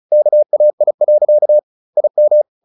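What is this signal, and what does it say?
Morse code, heard as a keyed tone, about 34 wpm, 600 Hz. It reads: KAI. IM
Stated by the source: Morse practice at 35 wpm from CW med Gnister, sending Ken Follett's novel Never in English.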